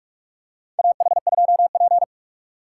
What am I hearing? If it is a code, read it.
Morse code, heard as a keyed tone, about 48 wpm, 700 Hz. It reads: AH2P
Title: USA Call Signs - 2x Character Spacing - 45wpm